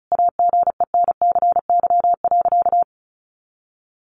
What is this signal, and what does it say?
Morse code, heard as a keyed tone, about 34 wpm, 700 Hz. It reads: AGENCY.
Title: Sentences from Top 1300 Words 35wpm